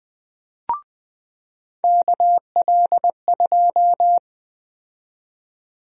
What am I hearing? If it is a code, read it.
Morse code, heard as a keyed tone, about 21 wpm, 700 Hz. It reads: KL2
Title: USA Call Sign Prefixes - 20wpm